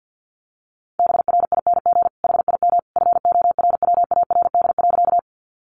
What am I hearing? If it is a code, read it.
Morse code, heard as a keyed tone, about 50 wpm, 700 Hz. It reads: BRING HIM FORWARD.
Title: Sentences from Top 1300 Words 50wpm